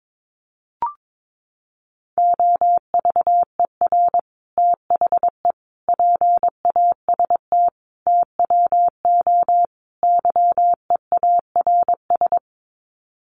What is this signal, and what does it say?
Morse code, heard as a keyed tone, about 22 wpm, 700 Hz. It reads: OVER THE PAST TWO YEARS
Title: Sets of 5 Words Encore 22wpm